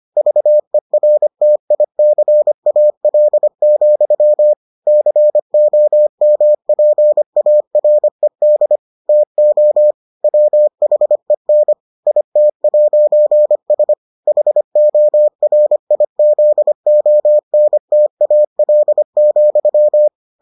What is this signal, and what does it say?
Morse code, heard as a keyed tone, about 25 wpm, 600 Hz. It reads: VERTICAL, COMPARED TO WHEN IT'S HORIZONTAL,